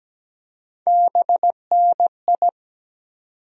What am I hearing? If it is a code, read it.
Morse code, heard as a keyed tone, about 17 wpm, 700 Hz. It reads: BNI
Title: USA Call Sign Postfixes - 3 Letters 17wpm